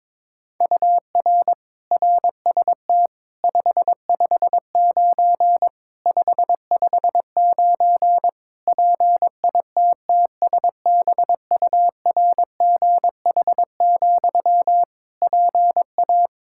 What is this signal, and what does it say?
Morse code, heard as a keyed tone, about 21 wpm, 700 Hz. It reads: UR RST 559 559 PITTSBURGH, PA
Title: Practice QSO - Exchange 1 - Beginning - 22wpm